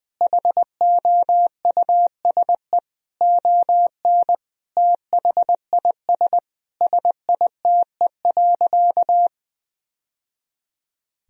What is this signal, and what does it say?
Morse code, heard as a keyed tone, about 20 wpm, 700 Hz. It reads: HOUSE ON THIS SITE.
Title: Sentences from Top 1000 Words - Review Part 4 - 20wpm